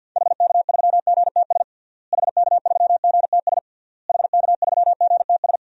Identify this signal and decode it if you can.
Morse code, heard as a keyed tone, about 53 wpm, 700 Hz. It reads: HK3GTS HK3GTS HK3GTS
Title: DXCC-100 Call Signs - 6 Characters - Repeated 3x - 50wpm